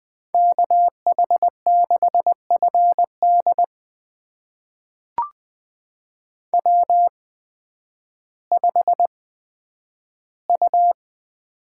Morse code, heard as a keyed tone, about 20 wpm, spaced as for 11 wpm, 700 Hz. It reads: KH6FD W5U
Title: USA Call Signs - 5 Characters - 8x Character Spacing - 20wpm